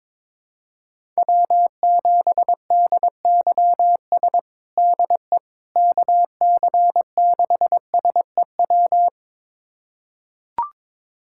Morse code, heard as a keyed tone, about 22 wpm, 700 Hz. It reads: W7DYS DE KC6SEW
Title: USA Callsign DE Callsign 22wpm